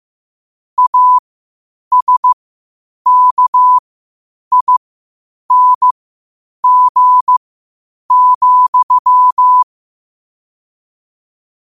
Morse code, heard as a keyed tone, about 15 wpm, spaced as for 8 wpm, 1000 Hz. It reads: ASKING,
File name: qotdfiles / 8wpm